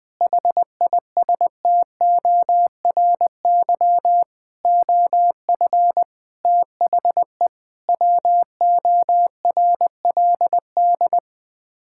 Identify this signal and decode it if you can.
Morse code, heard as a keyed tone, about 20 wpm, 700 Hz. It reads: HISTORY OF THE WORLD
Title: Sets of 6 Words - 20wpm